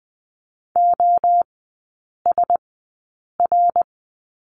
Morse code, heard as a keyed tone, about 20 wpm, spaced as for 8 wpm, 700 Hz. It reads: OSR
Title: 074 Lesson 13e - Letter L - Three Characters 20wpm